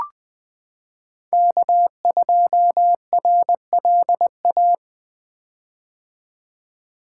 Morse code, heard as a keyed tone, about 20 wpm, 700 Hz. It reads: K2RLA
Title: Code Groups - 5 Letter:Numbers - 20wpm